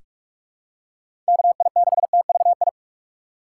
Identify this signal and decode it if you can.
Morse code, heard as a keyed tone, about 42 wpm, 700 Hz. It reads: KI6TVI
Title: USA Call Signs - 6 Characters - 4x Character Spacing - 45wpm